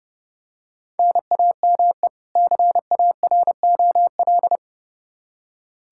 Morse code, heard as a keyed tone, about 31 wpm, 700 Hz. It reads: NAME CAROL
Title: Practice QSO - Exchange 1 - Beginning - 30wpm